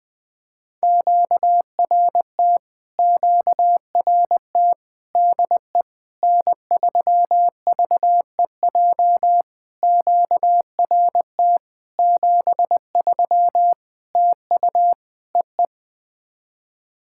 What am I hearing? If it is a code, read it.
Morse code, heard as a keyed tone, about 20 wpm, 700 Hz. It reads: QRT QRT DE N3VEJ QRT 73 TU EE
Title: POTA QSO Practice - 20wpm